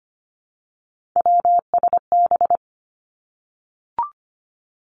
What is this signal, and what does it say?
Morse code, heard as a keyed tone, about 25 wpm, 700 Hz. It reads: WSB
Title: Three Challenging Letter-Number Combinations 25wpm